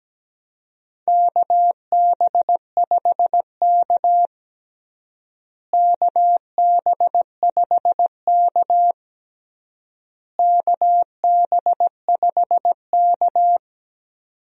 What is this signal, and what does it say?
Morse code, heard as a keyed tone, about 17 wpm, 700 Hz. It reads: KB5K KB5K KB5K
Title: USA Call Signs - 4 Characters - Repeated 3x - 17wpm